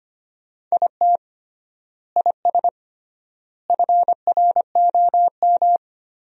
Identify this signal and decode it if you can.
Morse code, heard as a keyed tone, about 25 wpm, 700 Hz. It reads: IT IS FROM
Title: 3x Word-Spacing - Sentences from Top 300 Words - 25wpm